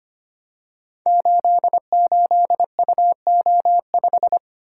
Morse code, heard as a keyed tone, about 25 wpm, 700 Hz. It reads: 88UO5
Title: Code Groups - 5 Letter:Numbers - 25wpm